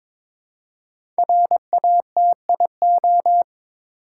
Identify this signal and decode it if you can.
Morse code, heard as a keyed tone, about 22 wpm, 700 Hz. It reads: RATIO
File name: 5 Letter Words - First 3 Letters Given - 22wpm